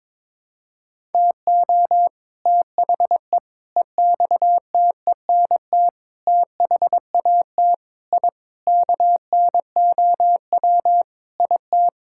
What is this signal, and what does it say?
Morse code, heard as a keyed tone, about 21 wpm, 700 Hz. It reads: TO THE EXTENT THAT I KNOW IT
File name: practice / 22wpm / Sentences with 4-Word Phrases - 22wpm